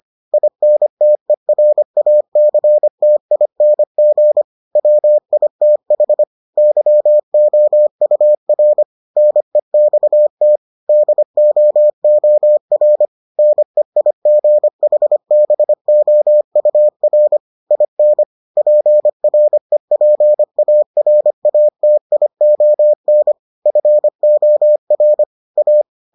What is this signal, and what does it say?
Morse code, heard as a keyed tone, about 25 wpm, 600 Hz. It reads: INTERACTING WITH YOUR NEXT DOOR NEIGHBOUR IN PREPARATION FOR A